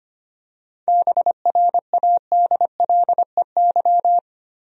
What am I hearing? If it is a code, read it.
Morse code, heard as a keyed tone, about 25 wpm, 700 Hz. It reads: BRADLEY
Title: USA Common Names - 25wpm